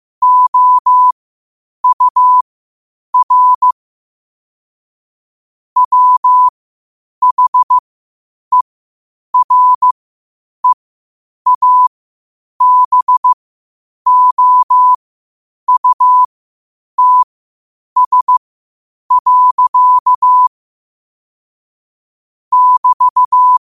Morse code, heard as a keyed tone, about 15 wpm, spaced as for 8 wpm, 1000 Hz. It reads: OUR WHEREABOUTS. <BT>
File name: qotdfiles / 8wpm